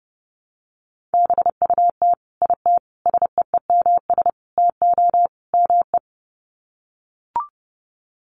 Morse code, heard as a keyed tone, about 28 wpm, 700 Hz. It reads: BUT IT SEEMS TO ME
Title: Sets of 5 Words Encore2 30wpm